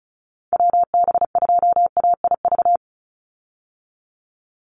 Morse code, heard as a keyed tone, about 34 wpm, 700 Hz. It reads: WB2AIV
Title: Call Signs Repeated 2x - 35wpm